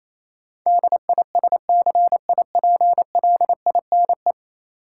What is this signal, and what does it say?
Morse code, heard as a keyed tone, about 28 wpm, 700 Hz. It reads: DISCIPLINE
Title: Sentences from Top 900 Words - New Word Drill - 28wpm